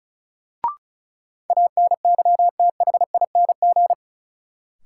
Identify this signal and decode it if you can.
Morse code, heard as a keyed tone, about 36 wpm, 700 Hz. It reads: ANYTHING
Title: Sentences from Top 1000 Words - New Word Drill - 35wpm